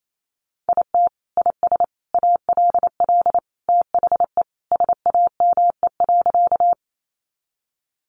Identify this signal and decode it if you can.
Morse code, heard as a keyed tone, about 28 wpm, 700 Hz. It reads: IT IS ALL THE SAME.
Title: Sentences from Top 200 Words 28wpm